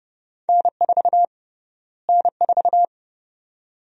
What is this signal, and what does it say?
Morse code, heard as a keyed tone, about 30 wpm, 700 Hz. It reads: N4 N4
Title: USA Call Sign Prefixes - Repeated 2x - 30wpm